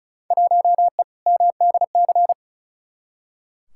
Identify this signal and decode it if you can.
Morse code, heard as a keyed tone, about 36 wpm, 690 Hz. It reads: 1E MDC
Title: Field Day Exchanges Encore - 35wpm